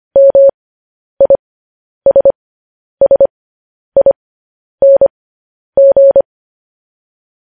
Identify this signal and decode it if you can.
Morse code, heard as a keyed tone, about 25 wpm, spaced as for 10 wpm, 550 Hz. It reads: MISSING
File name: podcast / News headlines-1x-25x10